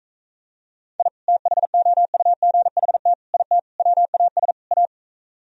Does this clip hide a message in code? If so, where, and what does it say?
Yes: Morse code from 1.0 to 4.9 s, heard as a keyed tone, about 45 wpm, 700 Hz. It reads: I THOUGHT IT WAS A